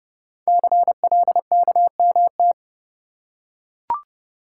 Morse code, heard as a keyed tone, about 30 wpm, 700 Hz. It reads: CLKMT
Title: Code Groups - 5 Letters - 30wpm